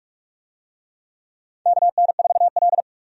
Morse code, heard as a keyed tone, about 45 wpm, 700 Hz. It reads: KN4L
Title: USA Call Signs - 4 Characters - 45wpm